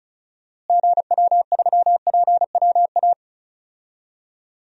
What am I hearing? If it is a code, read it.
Morse code, heard as a keyed tone, about 34 wpm, 700 Hz. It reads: GW3PWA
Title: DXCC-100 Call Signs - 6 Characters - Repeated 3x - 35wpm